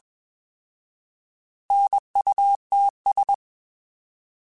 Morse code, heard as a keyed tone, about 21 wpm, 800 Hz. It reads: NUTS
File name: Alphabetical / Alphabetical mp3s / Ns @20